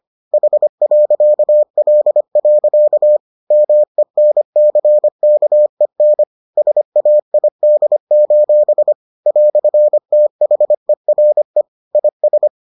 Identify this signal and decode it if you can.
Morse code, heard as a keyed tone, about 25 wpm, 600 Hz. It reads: H.L. MENCKEN SAID: "THERE IS